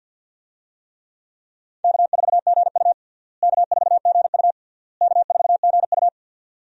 Morse code, heard as a keyed tone, about 52 wpm, 700 Hz. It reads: K4GU K4GU K4GU
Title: USA Call Signs - 4 Characters - Repeated 3x - 50wpm